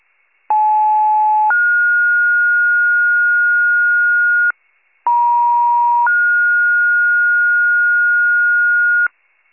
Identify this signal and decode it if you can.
Two-tone page, heard as a one-second tone, then a three-second tone, about 860 then 1430 Hz, about 950 then 1430 Hz, with noise. It is heard from 0.5 to 9.1 s.